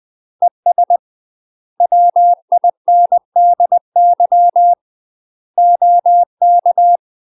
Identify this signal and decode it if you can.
Morse code, heard as a keyed tone, about 20 wpm, 700 Hz. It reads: ES WINDY OK
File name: UK MASTER QSOs TRAINING / 20WPM / UK MASTER QSOs 0011